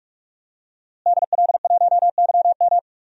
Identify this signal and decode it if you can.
Morse code, heard as a keyed tone, about 43 wpm, 700 Hz. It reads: DL1YM
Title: DXCC-100 Call Signs - 5 Characters - Repeated 2x - 45wpm